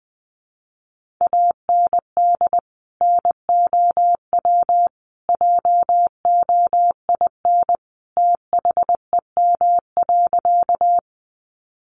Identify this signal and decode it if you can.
Morse code, heard as a keyed tone, about 19 wpm, 700 Hz. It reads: AND NOW JOIN THEM.
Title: Sentences from Top 1400 Words 20wpm